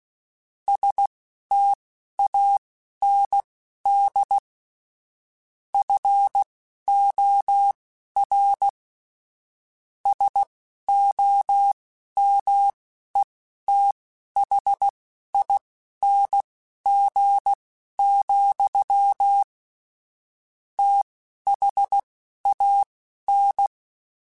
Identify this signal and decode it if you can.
Morse code, heard as a keyed tone, about 16 wpm, spaced as for 11 wpm, 800 Hz. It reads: STAND FOR SOMETHING, THAN
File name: Quotes / Quotes-mp3s / StandFor @10